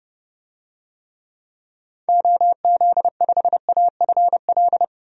Morse code, heard as a keyed tone, about 30 wpm, 700 Hz. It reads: OZ5AFL
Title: DXCC-100 Call Signs - 6 Characters - Repeated 3x - 30wpm